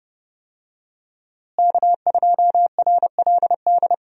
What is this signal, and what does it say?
Morse code, heard as a keyed tone, about 30 wpm, 700 Hz. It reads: K2RLD
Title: USA Call Signs - 5 Characters - Repeated 2x - 30wpm